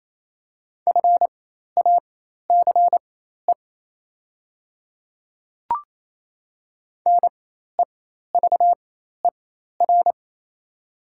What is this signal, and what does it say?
Morse code, heard as a keyed tone, about 27 wpm, spaced as for 13 wpm, 700 Hz. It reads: FACE NEVER
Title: 4x Character Spacing - Top 300 Words - 28wpm